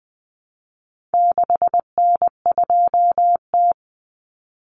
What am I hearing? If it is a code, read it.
Morse code, heard as a keyed tone, about 20 wpm, 700 Hz. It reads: 6N2T
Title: Common Cut Numbers - 4-Digit 20wpm